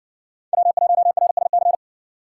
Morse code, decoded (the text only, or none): W1RSC